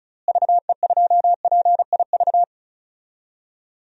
Morse code, heard as a keyed tone, about 34 wpm, 700 Hz. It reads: VE2PIV
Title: DXCC-100 Call Signs - 6 Characters - Repeated 3x - 35wpm